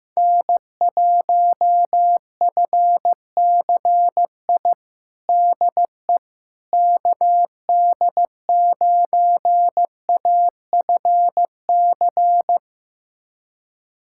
Morse code, heard as a keyed tone, about 15 wpm, 700 Hz. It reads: N1FCI DE KD9AFC
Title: USA Callsign DE Callsign 15wpm